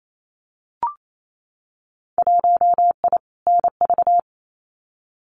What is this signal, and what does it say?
Morse code, heard as a keyed tone, about 28 wpm, 700 Hz. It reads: 1I NV
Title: Winter Field Day Exchanges Encore 28wpm